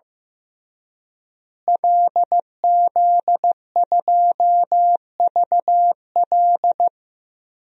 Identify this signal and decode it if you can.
Morse code, heard as a keyed tone, about 15 wpm, 700 Hz. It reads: LZ2VL
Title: DXCC-100 Call Signs - 5 Characters - Repeated 3x - 15wpm